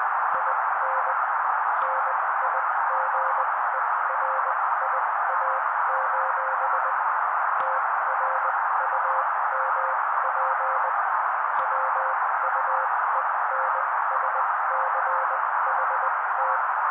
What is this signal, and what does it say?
Morse code, heard as a keyed tone, about 20 wpm, spaced as for 16 wpm, 530 Hz. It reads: IN NIGERIA: TRUMP WUENSCHT
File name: daily / 16wpm noise